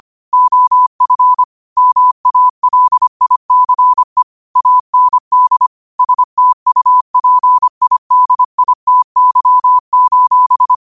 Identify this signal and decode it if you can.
Morse code, heard as a keyed tone, about 25 wpm, 1000 Hz. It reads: OF MALICE AND STUPIDITY:
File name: qotdfiles / 25wpm